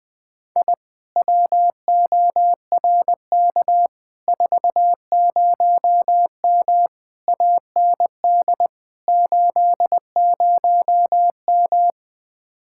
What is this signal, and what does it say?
Morse code, heard as a keyed tone, about 20 wpm, 700 Hz. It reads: I WORK 40M AND 80M
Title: Speed-Racing Common QSO Phrases - 17wpm